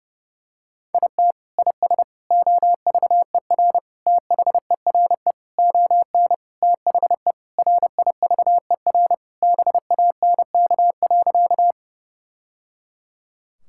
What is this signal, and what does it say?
Morse code, heard as a keyed tone, about 30 wpm, 700 Hz. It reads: IT IS OVER THERE ON THE RIVER BANK.
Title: Sentences from Top 1000 Words - Review Part 6 - 30wpm